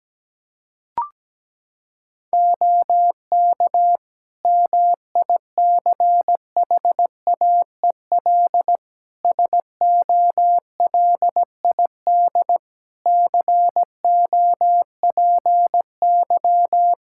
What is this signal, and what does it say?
Morse code, heard as a keyed tone, about 17 wpm, 700 Hz. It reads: OK MICHAEL SOLID COPY
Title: Practice QSO - Exchange 3 - More Info - 17wpm